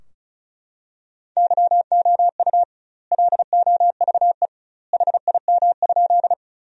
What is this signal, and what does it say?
Morse code, heard as a keyed tone, about 36 wpm, 700 Hz. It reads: YOU LOVE HIM?
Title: 2x Word-Spacing - Sentences from Top 400 Words - 35wpm